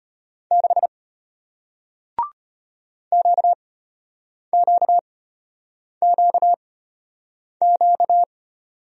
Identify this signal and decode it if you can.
Morse code, heard as a keyed tone, about 33 wpm, spaced as for 8 wpm, 700 Hz. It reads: 6 QQQQ